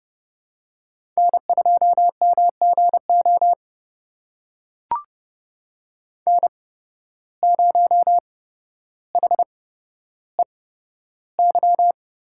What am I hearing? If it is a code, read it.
Morse code, heard as a keyed tone, about 30 wpm, spaced as for 13 wpm, 700 Hz. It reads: N2MGO N0HEY
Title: USA Call Signs - 5 Characters - 8x Character Spacing - 30wpm